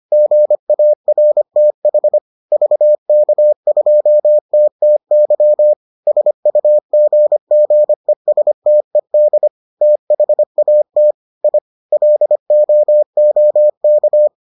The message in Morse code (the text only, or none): GARTH VK2TTY SUGGESTED THAT I LOOK